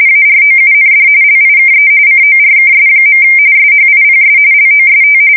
RTTY, about 45.45 baud. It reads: AROUND LIKE CORKS. SHIPS LOST
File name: fsk170 / battleshipnj